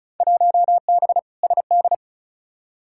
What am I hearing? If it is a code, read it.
Morse code, heard as a keyed tone, about 36 wpm, 700 Hz. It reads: 1B SD
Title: Field Day Exchanges Encore - 35wpm